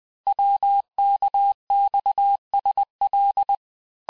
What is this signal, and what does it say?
Morse code, heard as a keyed tone, about 20 wpm, 800 Hz. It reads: WKXSL
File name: cours-du-lundi files / cours-1584-teste-chiffre-vitesse-19mpm